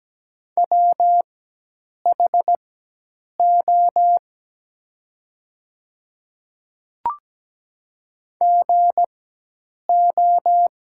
Morse code, heard as a keyed tone, about 17 wpm, spaced as for 8 wpm, 700 Hz. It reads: WHO GO
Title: 4x Character Spacing - Top 100 Words - 17wpm